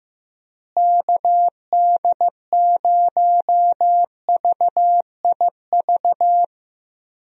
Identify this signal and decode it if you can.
Morse code, heard as a keyed tone, about 15 wpm, 700 Hz. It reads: KD0VIV
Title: USA Call Signs - 6 Characters - Repeated 3x - 15wpm